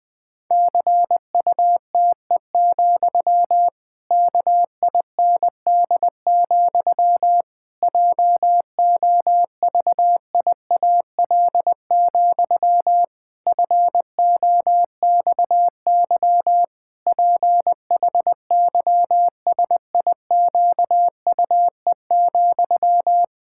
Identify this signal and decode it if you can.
Morse code, heard as a keyed tone, about 20 wpm, 700 Hz. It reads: CUTE, KIND, JOVIAL, FOXY PHYSIQUE,
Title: Pangram Challenge v2 - 20wpm